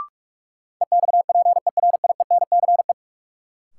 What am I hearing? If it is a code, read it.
Morse code, heard as a keyed tone, about 43 wpm, 700 Hz. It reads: EXPERIENCE
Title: Sentences from Top 600 Words - New Word Drill - 45wpm